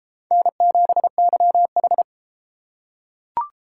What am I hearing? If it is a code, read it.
Morse code, heard as a keyed tone, about 31 wpm, 700 Hz. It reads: N7YH